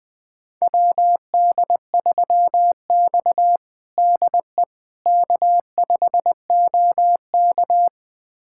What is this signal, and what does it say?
Morse code, heard as a keyed tone, about 20 wpm, 700 Hz. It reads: WD3X DE K5OK
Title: USA Callsign DE Callsign - 4 Characters - 20wpm